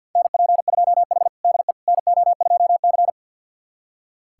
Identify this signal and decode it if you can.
Morse code, heard as a keyed tone, about 46 wpm, 700 Hz. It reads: NP2H DE NO2C
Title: USA Callsign DE Callsign - 4 Characters - 50wpm